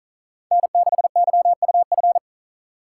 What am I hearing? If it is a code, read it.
Morse code, heard as a keyed tone, about 45 wpm, 700 Hz. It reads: N6YUF